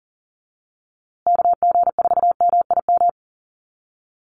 Morse code, heard as a keyed tone, about 40 wpm, 700 Hz. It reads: KG4MIM
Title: USA Call Signs Repeated 3x 40wpm